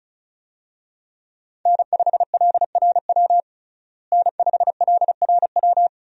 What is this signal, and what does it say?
Morse code, heard as a keyed tone, about 34 wpm, 700 Hz. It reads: N5LRW N5LRW
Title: USA Call Signs - 5 Characters - Repeated 2x - 35wpm